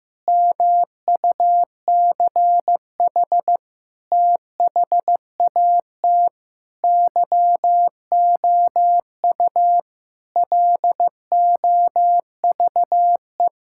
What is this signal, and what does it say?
Morse code, heard as a keyed tone, about 15 wpm, 700 Hz. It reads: MUCH THAT YOU LOVE
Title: Sentences from Top 1600 Words 15wpm